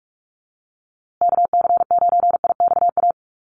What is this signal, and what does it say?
Morse code, heard as a keyed tone, about 44 wpm, 700 Hz. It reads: KC9IXA
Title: USA Call Signs Repeated 3x 45wpm